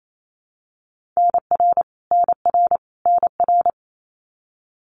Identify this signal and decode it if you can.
Morse code, heard as a keyed tone, about 28 wpm, 700 Hz. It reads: NR NR NR
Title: Abbreviations Repeated 3x - 28wpm